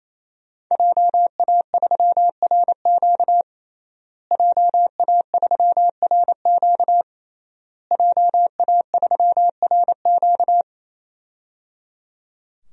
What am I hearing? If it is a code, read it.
Morse code, heard as a keyed tone, about 28 wpm, 700 Hz. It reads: JA3RQ JA3RQ JA3RQ
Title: DXCC-100 Call Signs - 5 Characters - Repeated 3x - 28wpm